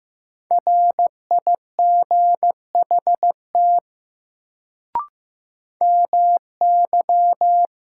Morse code, heard as a keyed tone, about 15 wpm, 700 Hz. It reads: RIGHT MY
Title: Sentences from Top 200 Words - New Word Drill - 15wpm